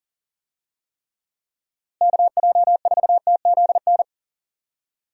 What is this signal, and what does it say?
Morse code, heard as a keyed tone, about 40 wpm, 700 Hz. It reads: KJ4TZN
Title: USA Call Signs - 6 Characters - Repeated 2x - 40wpm